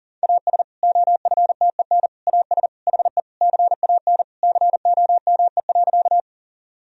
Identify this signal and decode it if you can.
Morse code, heard as a keyed tone, about 40 wpm, 700 Hz. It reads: AS OFTEN AS HE CAN COME.
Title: Sentences from Top 1000 Words - Review Part 1 - 40wpm